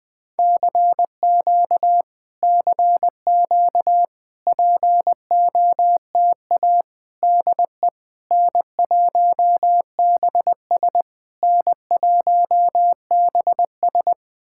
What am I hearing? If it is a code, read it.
Morse code, heard as a keyed tone, about 19 wpm, 700 Hz. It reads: CQ CQ POTA DE N1BS N1BS
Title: POTA QSO Practice - 20wpm